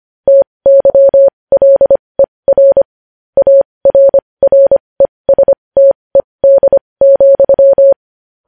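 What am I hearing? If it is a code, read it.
Morse code, heard as a keyed tone, about 25 wpm, spaced as for 20 wpm, 550 Hz. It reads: TYLER ARRESTED,
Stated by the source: Morse code news headlines.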